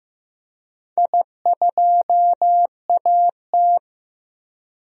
Morse code, heard as a keyed tone, about 15 wpm, 700 Hz. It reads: I2AT
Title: DXCC-100 Call Signs - 4 Characters - Repeated 2x - 15wpm